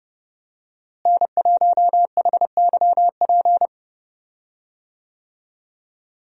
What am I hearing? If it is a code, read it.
Morse code, heard as a keyed tone, about 30 wpm, 700 Hz. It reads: N1HYP
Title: USA Call Signs - 5 Characters - Repeated 2x - 30wpm